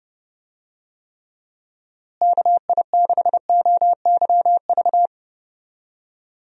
Morse code, heard as a keyed tone, about 31 wpm, 700 Hz. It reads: KI6OYV
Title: USA Call Signs - 6 Characters - Repeated 2x - 30wpm